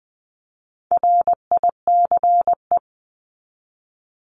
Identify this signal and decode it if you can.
Morse code, heard as a keyed tone, about 20 wpm, 700 Hz. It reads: RICE
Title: 260 Lesson 42e - Words 20-20wpm